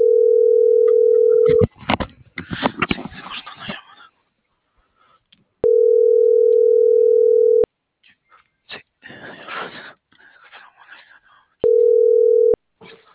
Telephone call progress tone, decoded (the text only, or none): ringback tone